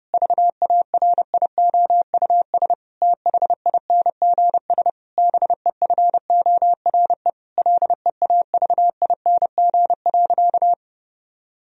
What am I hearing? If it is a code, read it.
Morse code, heard as a keyed tone, about 30 wpm, 700 Hz. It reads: VARIOUS THINGS BEFORE LEAVING.
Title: Sentences from Top 1000 Words - Review Part 4 - 30wpm